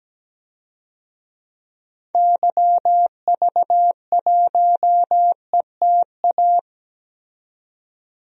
Morse code, heard as a keyed tone, about 17 wpm, 700 Hz. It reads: YV1ETA
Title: DXCC-100 Call Signs - 6 Characters - 17wpm